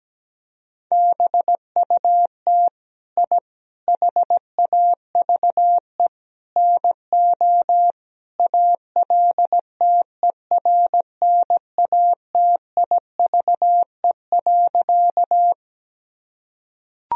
Morse code, heard as a keyed tone, about 17 wpm, 700 Hz. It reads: BUT I HAVE NO ALTERNATIVE.
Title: Sentences from Top 1000 Words - Review Part 4 - 17wpm